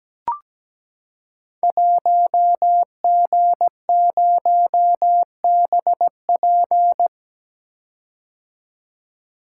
Morse code, heard as a keyed tone, about 17 wpm, 700 Hz. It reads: 1G0BP
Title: Code Groups - 5 Letter:Numbers - 17wpm